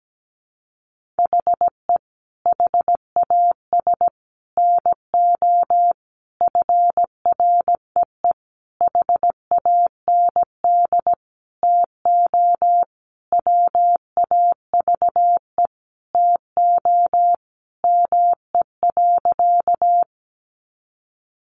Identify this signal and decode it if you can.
Morse code, heard as a keyed tone, about 17 wpm, 700 Hz. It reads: HE HAS NO FREE HAND TO WAVE TO ME.
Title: Sentences from Top 1400 Words 17wpm